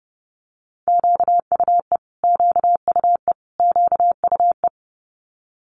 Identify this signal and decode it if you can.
Morse code, heard as a keyed tone, about 30 wpm, 700 Hz. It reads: QUE QUE QUE
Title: Multi-letter Phonemes Repeated 3x 30wpm